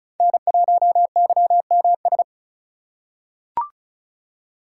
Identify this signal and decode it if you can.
Morse code, heard as a keyed tone, about 34 wpm, 700 Hz. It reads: N1YMS
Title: USA Call Signs - 5 Characters - 2x Character Spacing - 35wpm